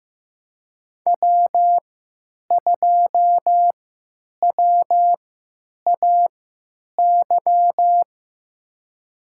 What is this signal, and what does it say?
Morse code, heard as a keyed tone, about 15 wpm, spaced as for 9 wpm, 700 Hz. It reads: W2WAY
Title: USA Call Signs - 5 Characters - 3x Character Spacing - 15wpm